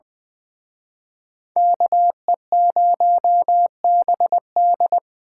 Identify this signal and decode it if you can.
Morse code, heard as a keyed tone, about 23 wpm, 700 Hz. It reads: KE0BD
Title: Speed Racing - USA 2x2 Call Signs 20wpm